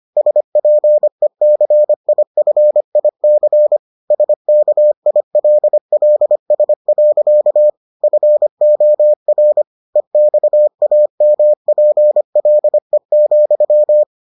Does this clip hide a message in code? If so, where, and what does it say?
Yes: Morse code from 0.2 to 14.0 s, heard as a keyed tone, about 25 wpm, 600 Hz. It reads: SPECIFIC SKILLS. FOR EXAMPLE,